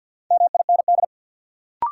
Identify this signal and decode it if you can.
Morse code, heard as a keyed tone, about 53 wpm, 700 Hz. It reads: MIND
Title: Sets of 5 Words - New Word Drill - 50wpm